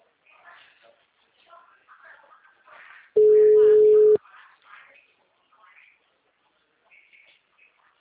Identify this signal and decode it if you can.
Ringback tone (Europe, 425 Hz), with speech.
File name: out- -2011- - - .8006